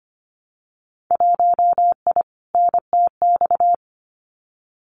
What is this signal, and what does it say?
Morse code, heard as a keyed tone, about 25 wpm, 700 Hz. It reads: 1I NTX
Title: Winter Field Day Exchanges 25wpm